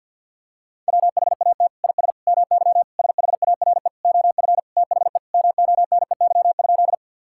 Morse code, heard as a keyed tone, about 53 wpm, 700 Hz. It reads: WHAT IS MY SHARE OF THE MONEY?
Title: Sentences from Top 1000 Words - Review Part 6 - 50wpm